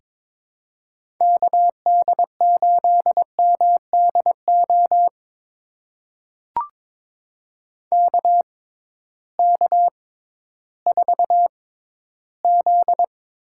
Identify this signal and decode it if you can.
Morse code, heard as a keyed tone, about 22 wpm, spaced as for 13 wpm, 700 Hz. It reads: KD8MDO KK4Z
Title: USA Call Signs - 6 Characters - 6x Character Spacing - 22wpm